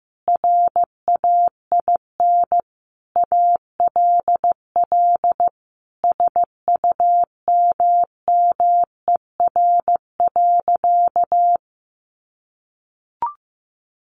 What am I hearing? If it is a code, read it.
Morse code, heard as a keyed tone, about 15 wpm, 700 Hz. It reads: RAIN ALL SUMMER.
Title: Sentences from Top 1100 Words 15wpm